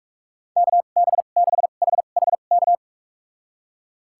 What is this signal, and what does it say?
Morse code, heard as a keyed tone, about 46 wpm, spaced as for 33 wpm, 700 Hz. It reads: KB6HHK
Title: USA Call Signs - 6 Characters - 2x Character Spacing - 45wpm